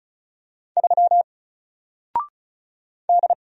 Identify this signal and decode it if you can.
Morse code, heard as a keyed tone, about 36 wpm, 700 Hz. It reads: 3 D
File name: Single-Letter-Number-35wpm